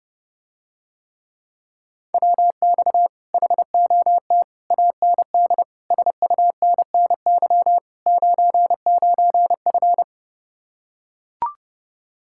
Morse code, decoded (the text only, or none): WX HOT AND SUNNY 99F